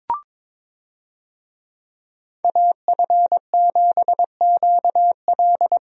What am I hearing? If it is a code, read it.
Morse code, heard as a keyed tone, about 22 wpm, 700 Hz. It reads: AF7QL
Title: USA Call Signs - 5 Characters - 22wpm